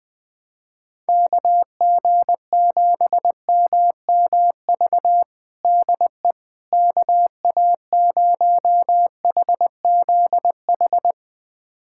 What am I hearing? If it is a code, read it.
Morse code, heard as a keyed tone, about 20 wpm, 700 Hz. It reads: KG7MMV DE KA0HZH
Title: USA Callsign DE Callsign 20wpm